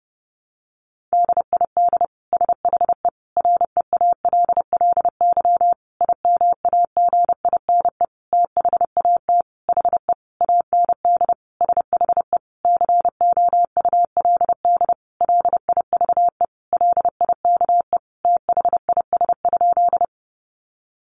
Morse code, heard as a keyed tone, about 30 wpm, 700 Hz. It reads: DID SHE REALLY IMAGINE THAT HE AND SHE COULD LIVE LIKE THIS?
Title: Sentences from Top 1400 Words 30wpm